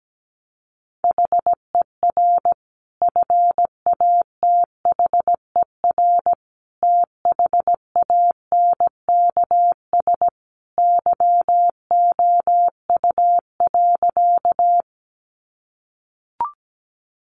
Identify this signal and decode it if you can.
Morse code, heard as a keyed tone, about 17 wpm, 700 Hz. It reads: HER FATHER THANKS YOU.
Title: Sentences from Top 600 Words 17wpm